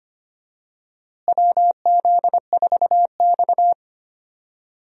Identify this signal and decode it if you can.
Morse code, heard as a keyed tone, about 25 wpm, 700 Hz. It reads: WZ4X
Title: USA Call Signs - 4 Characters - Repeated 3x - 25wpm